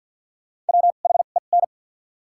Morse code, heard as a keyed tone, about 54 wpm, spaced as for 37 wpm, 700 Hz. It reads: WHEN
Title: 2x Character Spacing - Top 200 Words - 50wpm